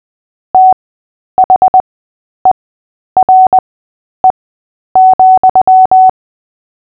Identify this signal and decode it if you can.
Morse code, heard as a keyed tone, about 20 wpm, spaced as for 10 wpm, 750 Hz. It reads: THERE,
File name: cw / fists / Short-Story-8-10wpm